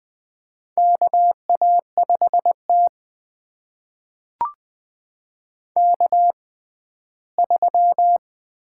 Morse code, heard as a keyed tone, about 20 wpm, spaced as for 13 wpm, 700 Hz. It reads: KA5T K3
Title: USA Call Signs - 4 Characters - 6x Character Spacing - 20wpm